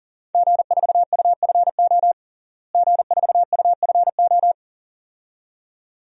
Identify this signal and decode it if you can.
Morse code, heard as a keyed tone, about 40 wpm, 700 Hz. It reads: G4UFO G4UFO
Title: DXCC-100 Call Signs - 5 Characters - Repeated 2x - 40wpm